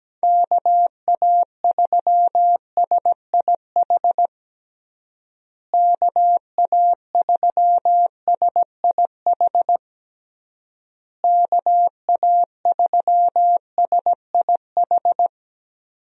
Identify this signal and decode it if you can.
Morse code, heard as a keyed tone, about 17 wpm, 700 Hz. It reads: KA3SIH KA3SIH KA3SIH
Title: USA Call Signs - 6 Characters - Repeated 3x - 17wpm